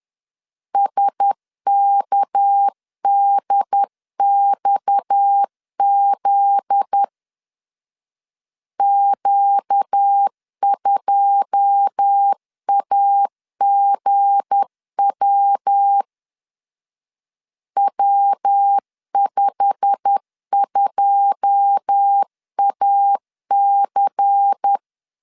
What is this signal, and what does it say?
Morse code, heard as a keyed tone, about 10 wpm, 800 Hz. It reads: SKDXZ Q2AGW W52AC
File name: LEZIONI / 9 2 10wpm 2736